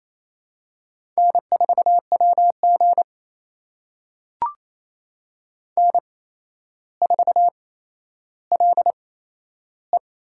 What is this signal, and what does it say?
Morse code, heard as a keyed tone, about 27 wpm, spaced as for 12 wpm, 700 Hz. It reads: N4WG N4LE
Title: USA Call Signs - 4 Characters - 8x Character Spacing - 28wpm